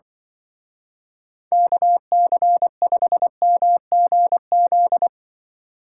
Morse code, heard as a keyed tone, about 25 wpm, 700 Hz. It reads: KC5MGZ
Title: Speed Racing - USA 2x3 Call Signs 20wpm